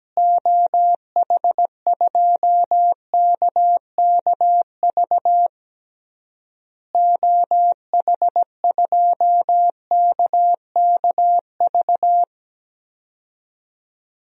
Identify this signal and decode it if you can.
Morse code, heard as a keyed tone, about 17 wpm, 700 Hz. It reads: OH2KKV OH2KKV
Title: DXCC-100 Call Signs - 6 Characters - Repeated 3x - 17wpm